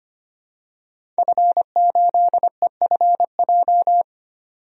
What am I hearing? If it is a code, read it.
Morse code, heard as a keyed tone, about 25 wpm, 700 Hz. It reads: F8EFJ
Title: DXCC-100 Call Signs - 5 Characters - 25wpm